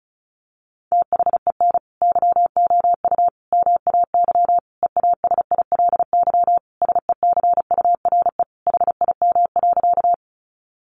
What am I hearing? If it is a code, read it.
Morse code, heard as a keyed tone, about 34 wpm, 700 Hz. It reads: THEN YOU MAY EASILY SECURE HIM.
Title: Sentences from Top 1200 Words 35wpm